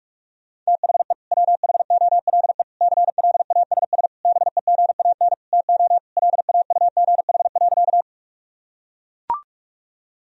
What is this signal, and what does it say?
Morse code, heard as a keyed tone, about 41 wpm, 700 Hz. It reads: THE WHOLE CLASS BEGAN TO LAUGH.
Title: Sentences from Top 1000 Words - Review Part 2 - 45wpm